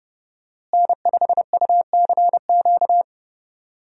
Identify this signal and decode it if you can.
Morse code, heard as a keyed tone, about 30 wpm, 700 Hz. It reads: N5UCQ
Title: USA Call Signs - 5 Characters - Repeated 3x - 30wpm